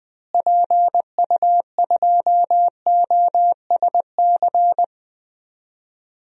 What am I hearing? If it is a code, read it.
Morse code, heard as a keyed tone, about 21 wpm, 700 Hz. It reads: PU2OSC
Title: DXCC-100 Call Signs - 6 Characters - Repeated 2x - 20wpm